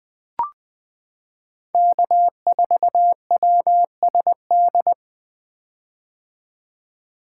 Morse code, heard as a keyed tone, about 20 wpm, 700 Hz. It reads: K4WSD
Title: 2023 POTA Activator Call Signs - 20wpm